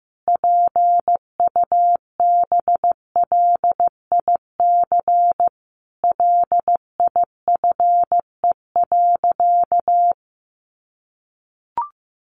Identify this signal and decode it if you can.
Morse code, heard as a keyed tone, about 15 wpm, 700 Hz. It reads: PUBLIC LIFE.
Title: Sentences from Top 600 Words 15wpm